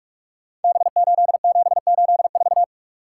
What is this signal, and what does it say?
Morse code, heard as a keyed tone, about 44 wpm, 700 Hz. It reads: B8784
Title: Code Groups - 5 Letter:Numbers - 45wpm